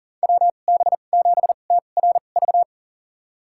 Morse code, heard as a keyed tone, about 40 wpm, spaced as for 29 wpm, 700 Hz. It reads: WB7TRV
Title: USA Call Signs - 6 Characters - 2x Character Spacing - 40wpm